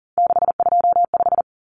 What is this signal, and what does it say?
Morse code, heard as a keyed tone, about 40 wpm, 700 Hz. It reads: 625
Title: Numbers - 3-Digit 40wpm